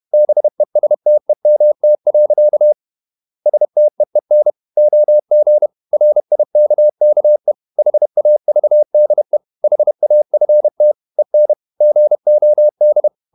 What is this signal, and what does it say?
Morse code, heard as a keyed tone, about 31 wpm, 600 Hz. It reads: BESTEMT. STEEN OG RIKKE HAVDE HAFT EN GOD
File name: Ferie QRQ 31wpm